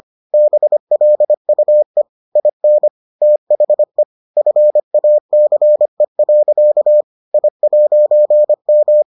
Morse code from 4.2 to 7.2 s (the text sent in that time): FACE.